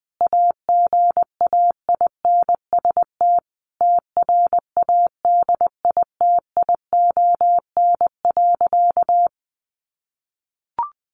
Morse code, heard as a keyed tone, about 20 wpm, 700 Hz. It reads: AGAINST TRADITION.
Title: Sentences from Top 1500 Words 20wpm